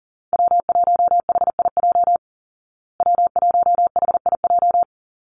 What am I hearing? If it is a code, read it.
Morse code, heard as a keyed tone, about 37 wpm, 700 Hz. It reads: W1HIJ W1HIJ
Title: Call Signs Repeated 2x - 40wpm